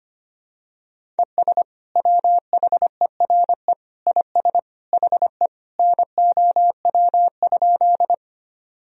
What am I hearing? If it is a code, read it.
Morse code, heard as a keyed tone, about 25 wpm, 700 Hz. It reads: ES WHERE IS HE NOW?
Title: Sentences from Top 200 Words and Abbreviations 25wpm